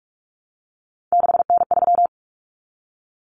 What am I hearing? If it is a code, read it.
Morse code, heard as a keyed tone, about 44 wpm, 700 Hz. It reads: 6N3
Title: Common Cut Numbers - 3-Digit 45wpm